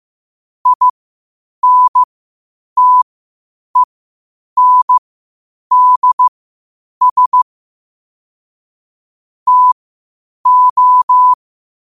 Morse code, heard as a keyed tone, about 15 wpm, spaced as for 8 wpm, 1000 Hz. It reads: INTENDS TO
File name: qotdfiles / 8wpm